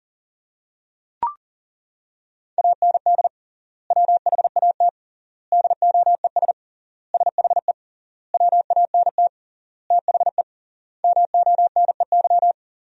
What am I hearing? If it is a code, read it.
Morse code, heard as a keyed tone, about 40 wpm, 700 Hz. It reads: AND WHAT DOES SHE WANT THE MONEY
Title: 3x Word-Spacing - Sentences from Top 400 Words - 40wpm